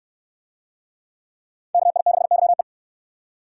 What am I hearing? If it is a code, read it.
Morse code, heard as a keyed tone, about 60 wpm, 700 Hz. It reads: KI6YE